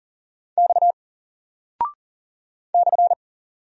Morse code, heard as a keyed tone, about 38 wpm, 700 Hz. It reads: X /
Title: Single Character (Letter:Number:Punctuation:Prosign) 40wpm